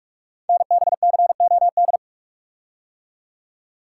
Morse code, heard as a keyed tone, about 44 wpm, 690 Hz. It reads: NBCOD
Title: Code Groups - 5 Letters - 45wpm